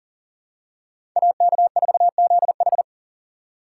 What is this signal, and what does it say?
Morse code, heard as a keyed tone, about 40 wpm, 700 Hz. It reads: AK4ZH
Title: USA Call Signs - 5 Characters - 2x Character Spacing - 40wpm